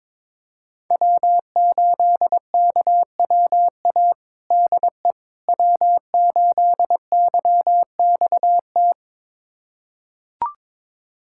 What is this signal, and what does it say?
Morse code, heard as a keyed tone, about 22 wpm, 700 Hz. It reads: W8KWA DE W8YXT
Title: USA Callsign DE Callsign - 5 Characters - 22wpm